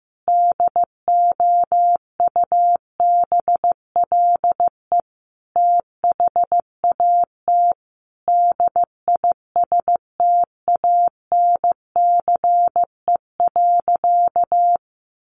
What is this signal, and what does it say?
Morse code, heard as a keyed tone, about 14 wpm, 700 Hz. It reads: DOUBLE THAT DISTANCE.
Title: Sentences from Top 1400 Words 15wpm